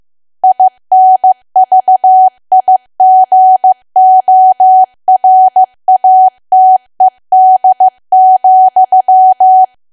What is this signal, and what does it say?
Morse code, heard as a keyed tone, about 15 wpm, 750 Hz. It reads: INVIGORATED,
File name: cwa / W1AW-Level2 / 15WPM